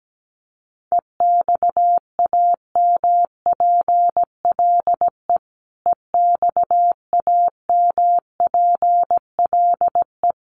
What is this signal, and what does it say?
Morse code, heard as a keyed tone, about 17 wpm, 700 Hz. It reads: EXAMPLE EXAMPLE
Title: Top 500 Words Repeated 17wpm